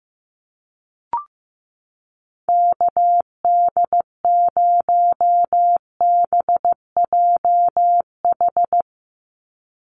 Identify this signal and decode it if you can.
Morse code, heard as a keyed tone, about 15 wpm, 700 Hz. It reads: KD0BJH
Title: USA Call Signs 15wpm